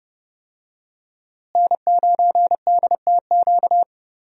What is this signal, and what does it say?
Morse code, heard as a keyed tone, about 30 wpm, 700 Hz. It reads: N9DTQ
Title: USA Call Signs - 5 Characters - Repeated 3x - 30wpm